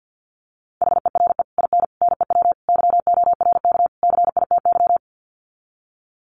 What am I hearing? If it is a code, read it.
Morse code, heard as a keyed tone, about 50 wpm, 700 Hz. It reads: HERE IN NEW YORK CITY
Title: Sets of 5 Words Encore2 50wpm